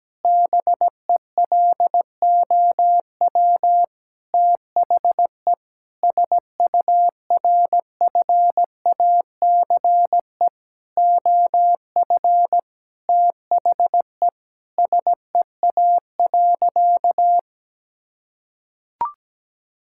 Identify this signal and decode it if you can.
Morse code, heard as a keyed tone, about 17 wpm, 700 Hz. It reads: BELOW THE SURFACE OF THE SEA.
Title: Sentences from Top 1000 Words - Review Part 5 - 17wpm